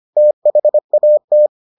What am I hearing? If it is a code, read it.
Morse code, heard as a keyed tone, about 25 wpm, 600 Hz. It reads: THAT